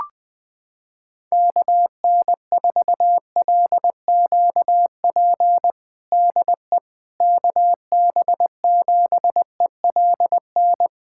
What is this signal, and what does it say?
Morse code, heard as a keyed tone, about 21 wpm, 700 Hz. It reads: KN4LQP DE KB7ELN
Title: USA Callsign DE Callsign 20wpm